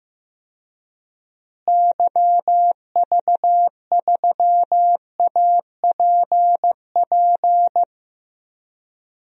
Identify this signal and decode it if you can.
Morse code, heard as a keyed tone, about 15 wpm, 700 Hz. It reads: YV3APP
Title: DXCC-100 Call Signs - 6 Characters - Repeated 2x - 15wpm